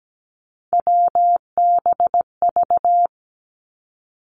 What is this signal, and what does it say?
Morse code, heard as a keyed tone, about 17 wpm, 700 Hz. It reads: WBV
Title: Three Challenging Letter-Number Combinations 17wpm